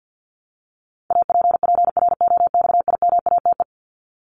Morse code, heard as a keyed tone, about 48 wpm, 700 Hz. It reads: APPROXIMATE
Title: LY Suffix - Top 100 Words with Base Words 50wpm